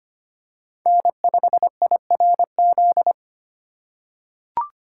Morse code, heard as a keyed tone, about 25 wpm, 700 Hz. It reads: N5IRZ
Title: USA Call Signs - 5 Characters - 3x Character Spacing - 25wpm